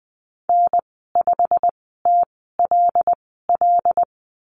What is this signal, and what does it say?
Morse code, heard as a keyed tone, about 20 wpm, spaced as for 14 wpm, 700 Hz. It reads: N5TLL
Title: 167 Lesson 27g - Number 7 - Call Signs 20wpm